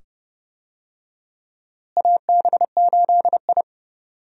Morse code, heard as a keyed tone, about 30 wpm, 700 Hz. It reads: AB8I
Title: USA Call Signs - 4 Characters - Repeated 2x - 30wpm